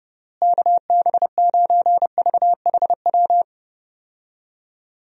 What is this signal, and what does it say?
Morse code, heard as a keyed tone, about 30 wpm, 700 Hz. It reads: KB9VHW
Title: USA Call Signs - 6 Characters - Repeated 2x - 30wpm